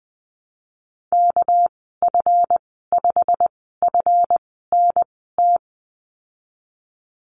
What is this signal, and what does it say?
Morse code, heard as a keyed tone, about 20 wpm, spaced as for 15 wpm, 700 Hz. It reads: KF5FNT
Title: 207 Lesson 33g - Letter j - Call Signs 20wpm